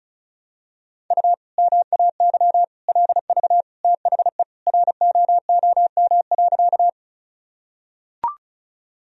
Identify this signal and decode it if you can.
Morse code, heard as a keyed tone, about 34 wpm, 700 Hz. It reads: U MAY LV THE ROOM.
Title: Sentences from Top 400 Words and Abbreviations 35wpm